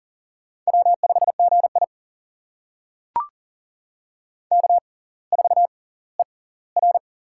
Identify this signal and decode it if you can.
Morse code, heard as a keyed tone, about 39 wpm, spaced as for 20 wpm, 690 Hz. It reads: W5GI K4ER
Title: USA Call Signs - 4 Characters - 6x Character Spacing - 40wpm